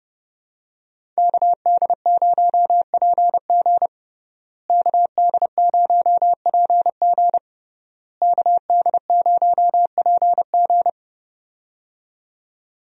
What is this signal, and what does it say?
Morse code, heard as a keyed tone, about 29 wpm, 700 Hz. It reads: KD0PG KD0PG KD0PG
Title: USA Call Signs - 5 Characters - Repeated 3x - 30wpm